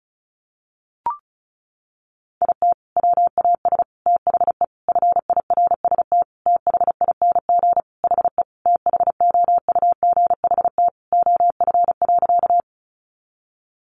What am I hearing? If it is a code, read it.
Morse code, heard as a keyed tone, about 34 wpm, 700 Hz. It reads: IT WAS THE FIRST THING HE THOUGHT OF.
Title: Sentences from Top 300 Words 35wpm